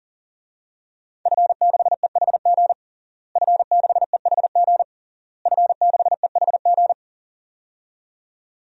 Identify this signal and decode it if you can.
Morse code, heard as a keyed tone, about 40 wpm, 700 Hz. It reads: F6EHG F6EHG F6EHG
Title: DXCC-100 Call Signs - 5 Characters - Repeated 3x - 40wpm